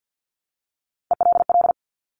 Morse code, heard as a keyed tone, about 50 wpm, 700 Hz. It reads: ELL
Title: Top Three Letter Combinations in Words 50wpm